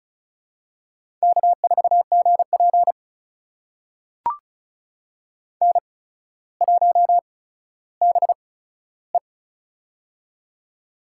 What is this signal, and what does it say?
Morse code, heard as a keyed tone, about 36 wpm, spaced as for 15 wpm, 700 Hz. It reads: K4GP N1BE
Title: USA Call Signs - 4 Characters - 8x Character Spacing - 35wpm